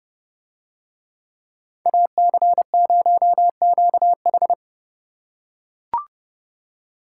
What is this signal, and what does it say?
Morse code, heard as a keyed tone, about 30 wpm, 700 Hz. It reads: AC0QH